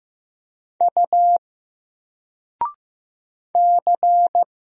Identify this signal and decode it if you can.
Morse code, heard as a keyed tone, about 15 wpm, 700 Hz. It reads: U C
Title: Single Letter-Number Alt 15wpm